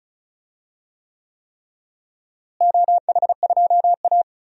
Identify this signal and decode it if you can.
Morse code, heard as a keyed tone, about 34 wpm, 700 Hz. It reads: OH2A
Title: DXCC-100 Call Signs - 4 Characters - Repeated 3x - 35wpm